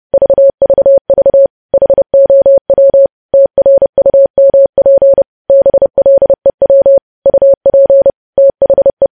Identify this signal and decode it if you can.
Morse code, heard as a keyed tone, about 30 wpm, 550 Hz. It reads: VVV HOW TRUMP BLEW UP THE
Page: news headlines in morse code at 30 wpm